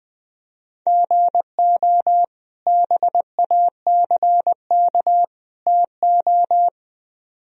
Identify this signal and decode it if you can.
Morse code, heard as a keyed tone, about 20 wpm, 700 Hz. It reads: GO BACK TO
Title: Speed-Racing - Sets of 3 Words 17wpm